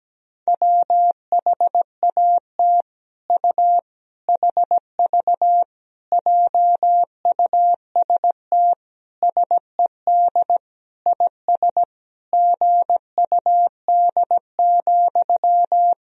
Morse code, heard as a keyed tone, about 17 wpm, 700 Hz. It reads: WHAT U HV JUST SED IS GUD,
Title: Sentences from Top 200 Words and Abbreviations 17wpm